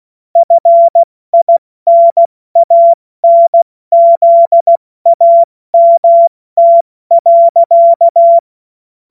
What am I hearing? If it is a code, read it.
Morse code, heard as a keyed tone, about 16 wpm, 680 Hz. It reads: FINANZAMT.
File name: Witz-28-Oktober-2025-70-F80